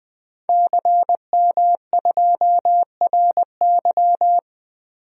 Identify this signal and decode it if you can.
Morse code, heard as a keyed tone, about 20 wpm, 700 Hz. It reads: CM2RY
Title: DXCC-100 Call Signs - 5 Characters - 20wpm